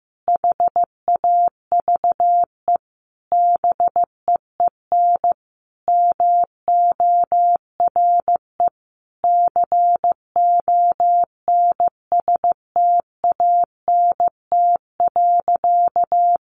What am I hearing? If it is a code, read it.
Morse code, heard as a keyed tone, about 15 wpm, 700 Hz. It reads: HAVE BEEN MORE CONSTANT.
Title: Sentences from Top 1500 Words 15wpm